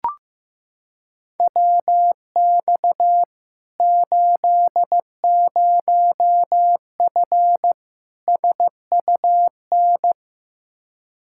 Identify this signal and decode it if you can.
Morse code, heard as a keyed tone, about 15 wpm, 700 Hz. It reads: WX 80F SUN
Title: Practice QSO - Exchange 2 - Info - 15wpm